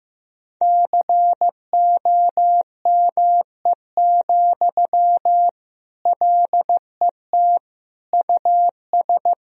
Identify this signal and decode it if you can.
Morse code, heard as a keyed tone, about 15 wpm, 700 Hz. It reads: COME, LET US
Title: Sentences from Top 1000 Words - Review Part 3 - 15wpm